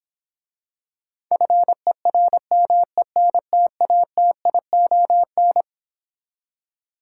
Morse code, heard as a keyed tone, about 26 wpm, 700 Hz. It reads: FERMENTATION